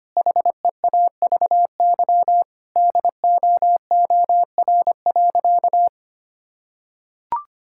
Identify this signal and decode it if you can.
Morse code, heard as a keyed tone, about 25 wpm, 700 Hz. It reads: HEAVY DOOR.
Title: Sentences from Top 1600 Words 25wpm